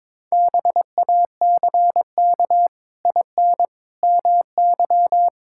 Morse code, heard as a keyed tone, about 22 wpm, 700 Hz. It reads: BACK IN MY
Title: Pangram Challenge v2 - 22wpm